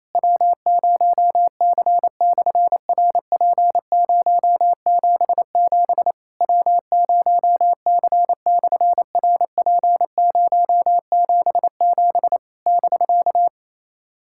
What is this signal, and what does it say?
Morse code, heard as a keyed tone, about 28 wpm, 700 Hz. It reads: W0C/RP077 W0C/RP077 <BK>
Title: SOTA QSO Practice 28wpm